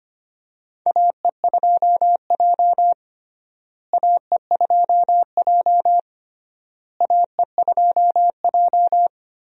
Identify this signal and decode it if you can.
Morse code, heard as a keyed tone, about 25 wpm, 700 Hz. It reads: AE2J AE2J AE2J
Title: USA Call Signs - 4 Characters - Repeated 3x - 25wpm